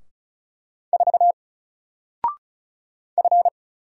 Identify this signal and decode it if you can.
Morse code, heard as a keyed tone, about 36 wpm, 700 Hz. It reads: 4 F
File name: Single-Letter-Number-35wpm